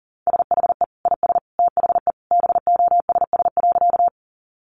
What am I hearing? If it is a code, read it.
Morse code, heard as a keyed tone, about 40 wpm, 700 Hz. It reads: SHE IS THE BOSS.
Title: Sentences from Top 700 Words 40wpm